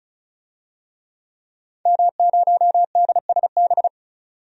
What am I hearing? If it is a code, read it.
Morse code, heard as a keyed tone, about 36 wpm, 700 Hz. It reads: M0DSB
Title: DXCC-100 Call Signs - 5 Characters - 35wpm